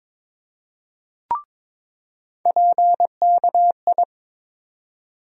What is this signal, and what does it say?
Morse code, heard as a keyed tone, about 22 wpm, 700 Hz. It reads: PKI
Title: USA Call Sign Postfixes - 3 Letters 22wpm